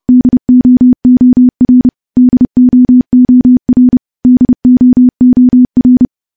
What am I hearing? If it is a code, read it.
Morse code, heard as a keyed tone, about 30 wpm, 250 Hz. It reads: DOOR DOOR DOOR
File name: morse / morse-training-30WPM